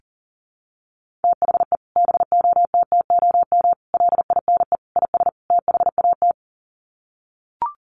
Morse code, heard as a keyed tone, about 40 wpm, 700 Hz. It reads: THE BOTTOM LINE IS THAT
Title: Sets of 5 Words Encore 40wpm